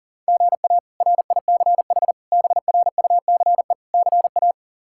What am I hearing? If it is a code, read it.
Morse code, heard as a keyed tone, about 37 wpm, 700 Hz. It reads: GA RICH BRUCE CA
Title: K1USN SST QSO Practice - 40wpm